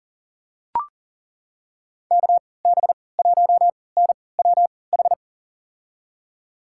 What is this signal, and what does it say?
Morse code, heard as a keyed tone, about 40 wpm, spaced as for 23 wpm, 700 Hz. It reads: KB1NWH
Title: USA Call Signs - 3x Character Spacing - 40wpm